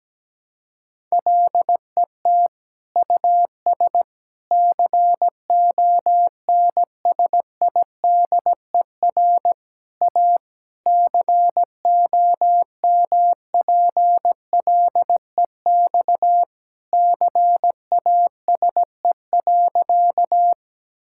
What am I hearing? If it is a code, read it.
Morse code, heard as a keyed tone, about 17 wpm, 700 Hz. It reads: LET US CONSIDER A COMPLEX CASE.
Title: Sentences from Top 1000 Words - Review Part 6 - 17wpm